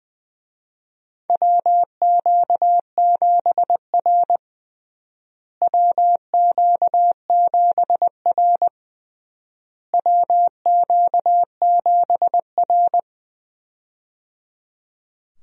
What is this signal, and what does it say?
Morse code, heard as a keyed tone, about 20 wpm, 700 Hz. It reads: WQ7R WQ7R WQ7R
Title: USA Call Signs - 4 Characters - Repeated 3x - 20wpm